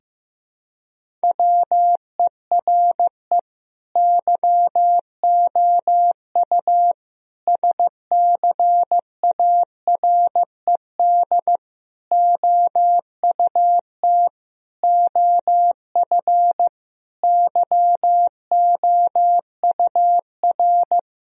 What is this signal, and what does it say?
Morse code, heard as a keyed tone, about 15 wpm, 700 Hz. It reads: WERE YOU SCARED OUT OF YOUR
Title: Sentences from Top 1000 Words - Review Part 4 - 15wpm